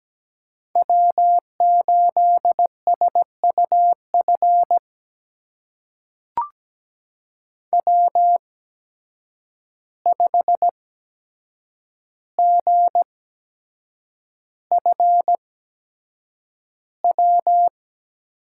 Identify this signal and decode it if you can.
Morse code, heard as a keyed tone, about 17 wpm, spaced as for 7 wpm, 700 Hz. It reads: W8SUF W5GFW
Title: USA Call Signs - 5 Characters - 8x Character Spacing - 17wpm